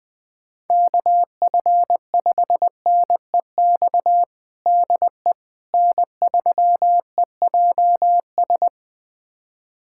Morse code, heard as a keyed tone, about 21 wpm, 700 Hz. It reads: KF5NEX DE N3EJS
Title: USA Callsign DE Callsign 20wpm